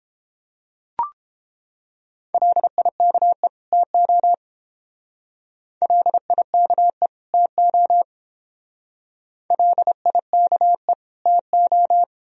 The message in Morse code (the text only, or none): LIKE TO LIKE TO LIKE TO